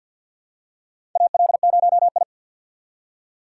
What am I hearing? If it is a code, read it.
Morse code, heard as a keyed tone, about 49 wpm, 700 Hz. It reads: AL0I
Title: USA Call Signs - 3x Character Spacing - 50wpm